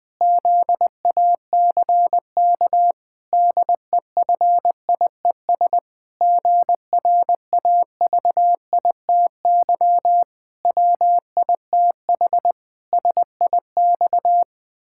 Morse code, heard as a keyed tone, about 20 wpm, 700 Hz. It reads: ZACK DEFIES GRAVITY WITH SIX
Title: Pangram Challenge v2 - 20wpm